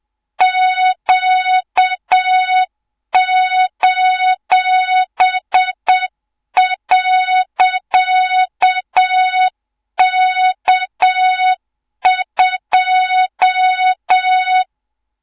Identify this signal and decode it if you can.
Morse code, heard as a keyed tone, about 7 wpm, 3700 Hz, with noise. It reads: Q:.K2
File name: Morze-HA2MN / hang / 24 GYAKORLAT